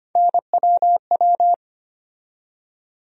Morse code, heard as a keyed tone, about 25 wpm, 700 Hz. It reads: NWW
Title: USA Call Sign Postfixes - 3 Letters 25wpm